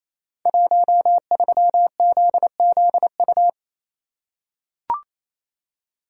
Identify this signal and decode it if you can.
Morse code, heard as a keyed tone, about 28 wpm, 700 Hz. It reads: 13ZZU
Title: Code Groups - 5 Letter:Numbers - 28wpm